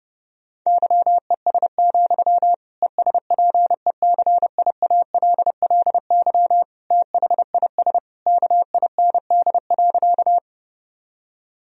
Morse code, heard as a keyed tone, about 30 wpm, 700 Hz. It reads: YES, ESPECIALLY THIS KIND.
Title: Sentences from Top 1000 Words - Review Part 4 - 30wpm